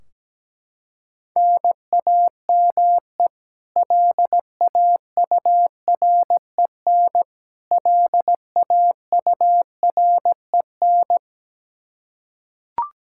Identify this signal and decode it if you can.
Morse code, heard as a keyed tone, about 17 wpm, 700 Hz. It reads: NAME LAUREN LAUREN
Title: Practice QSO - Exchange 1 - Beginning - 17wpm